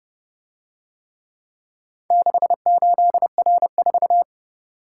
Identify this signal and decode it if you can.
Morse code, heard as a keyed tone, about 32 wpm, 700 Hz. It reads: 68R4